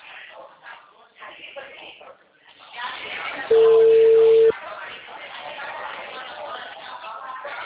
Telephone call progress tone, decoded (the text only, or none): ringback tone